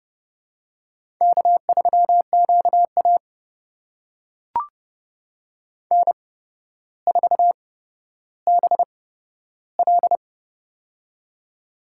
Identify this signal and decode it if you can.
Morse code, heard as a keyed tone, about 28 wpm, spaced as for 13 wpm, 700 Hz. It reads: K3QA N4BL
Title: USA Call Signs - 4 Characters - 8x Character Spacing - 30wpm